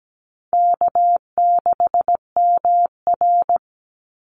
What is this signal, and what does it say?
Morse code, heard as a keyed tone, about 17 wpm, 700 Hz. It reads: K6MR
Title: Easy Call Signs - 17wpm